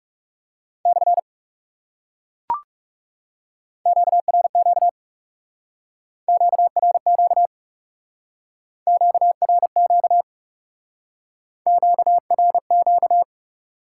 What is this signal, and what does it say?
Morse code, heard as a keyed tone, about 37 wpm, 700 Hz. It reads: / QRQ QRQ QRQ QRQ
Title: Speed-Racing Top 180 QSO Elements 30wpm